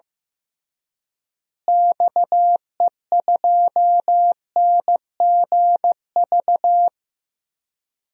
Morse code, heard as a keyed tone, about 15 wpm, 700 Hz. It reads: XE2NGV
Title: DXCC-100 Call Signs - 6 Characters - Repeated 3x - 15wpm